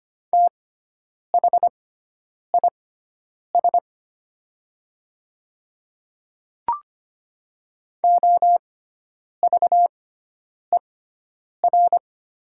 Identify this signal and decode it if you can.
Morse code, heard as a keyed tone, about 25 wpm, spaced as for 9 wpm, 700 Hz. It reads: THIS OVER
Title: 6x Character Spacing - Top 100 Words - 25wpm